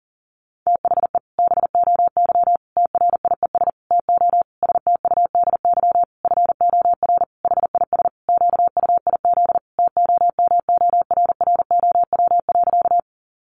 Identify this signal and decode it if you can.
Morse code, heard as a keyed tone, about 40 wpm, 700 Hz. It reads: THE BOY TRIES TO STUDY FOR HIS QUIZ TOMORROW.
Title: Additional Easy Phrases 40wpm